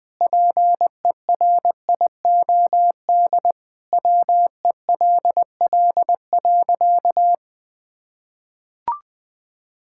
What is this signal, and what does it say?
Morse code, heard as a keyed tone, about 20 wpm, 700 Hz. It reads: PERIOD WELL.
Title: Sentences from Top 1000 Words - Review Part 5 - 20wpm